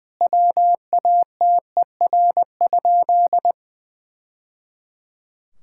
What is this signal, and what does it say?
Morse code, heard as a keyed tone, about 21 wpm, 700 Hz. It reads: WATER?
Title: Sentences from Top 1000 Words - Review Part 5 - 20wpm